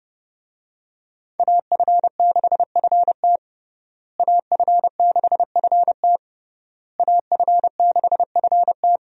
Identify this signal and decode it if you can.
Morse code, heard as a keyed tone, about 30 wpm, 700 Hz. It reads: AF6FT AF6FT AF6FT
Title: USA Call Signs - 5 Characters - Repeated 3x - 30wpm